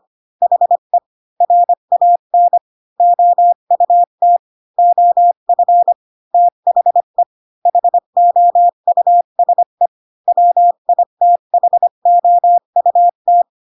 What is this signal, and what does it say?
Morse code, heard as a keyed tone, about 25 wpm, 700 Hz. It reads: HE RAN OUT OF THE HOUSE WITHOUT
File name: Pinocchio / Pinocchio ch8 20 25 ik1hge